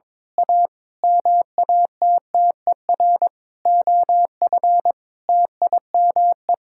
Morse code, heard as a keyed tone, about 22 wpm, 700 Hz. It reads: A MATTER OF TIME
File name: practice / 22wpm / Sentences with 8-Word Phrases - 22wpm